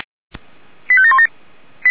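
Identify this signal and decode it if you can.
Five-tone selective call (ZVEI), heard as quick runs of five tones, tones of about 70 ms. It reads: 85717
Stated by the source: Received by SDR.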